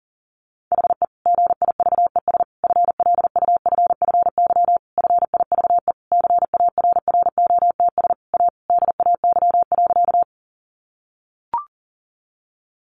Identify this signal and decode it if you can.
Morse code, heard as a keyed tone, about 40 wpm, 700 Hz. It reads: HE GIVES FLUFFY FIVE CARROTS A DAY.
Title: Easy Phrases 40wpm